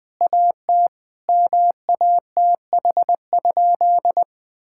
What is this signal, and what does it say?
Morse code, heard as a keyed tone, about 19 wpm, 700 Hz. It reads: AT MATH?
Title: Sentences from Top 1000 Words - Review Part 6 - 20wpm